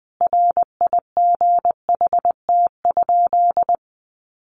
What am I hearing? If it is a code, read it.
Morse code, heard as a keyed tone, about 20 wpm, 700 Hz. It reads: RIGHT?
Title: Sentences from Top 1200 Words 20wpm